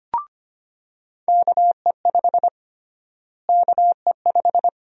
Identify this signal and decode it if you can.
Morse code, heard as a keyed tone, about 25 wpm, 700 Hz. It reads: KE5 KE5
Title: USA Call Sign Prefixes - Repeated 2x - 25wpm